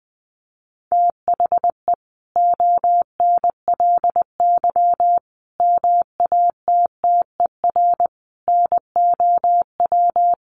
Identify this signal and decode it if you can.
Morse code, heard as a keyed tone, about 20 wpm, 700 Hz. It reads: THE ONLY MATTER NOW
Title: Sentences from Top 1400 Words 20wpm